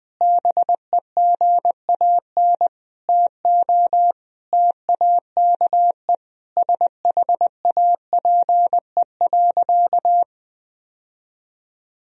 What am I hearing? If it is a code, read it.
Morse code, heard as a keyed tone, about 20 wpm, 700 Hz. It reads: BEGAN TO TAKE SHAPE.
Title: Sentences from Top 1000 Words - Review Part 3 - 20wpm